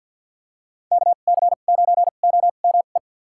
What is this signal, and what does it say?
Morse code, heard as a keyed tone, about 47 wpm, spaced as for 35 wpm, 700 Hz. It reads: KC9OME
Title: USA Call Signs - 2x Character Spacing - 50wpm